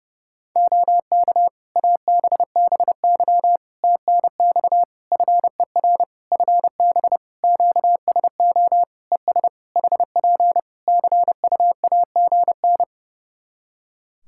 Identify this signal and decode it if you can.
Morse code, heard as a keyed tone, about 28 wpm, 700 Hz. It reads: OK ABBY TNX FER FB QSO ES HP CUAGN
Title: Practice QSO - Exchange 4 - Ending - 30wpm